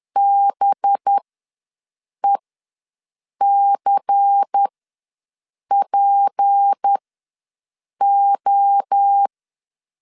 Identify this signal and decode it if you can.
Morse code, heard as a keyed tone, about 10 wpm, spaced as for 6 wpm, 800 Hz. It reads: BECPO
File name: LEZIONI / 6 PKX0 / 6 PKX ws 10wpm 3706